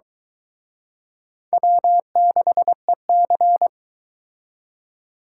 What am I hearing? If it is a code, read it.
Morse code, heard as a keyed tone, about 23 wpm, 700 Hz. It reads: W6EC